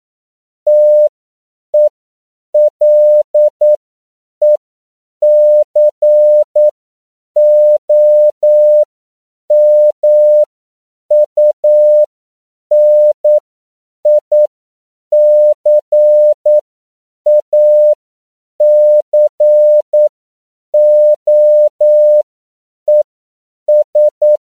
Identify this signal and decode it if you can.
Morse code, heard as a keyed tone, about 9 wpm, spaced as for 7 wpm, 600 Hz. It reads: TELECOMUNICACOES
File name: CW / 10PPM